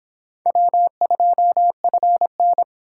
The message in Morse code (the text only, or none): W2FN